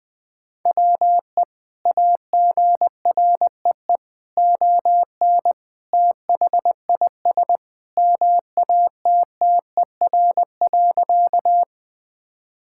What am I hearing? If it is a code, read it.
Morse code, heard as a keyed tone, about 20 wpm, 700 Hz. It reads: WE AGREE ON THIS MATTER.
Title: Sentences with 2-Word Phrases - 20wpm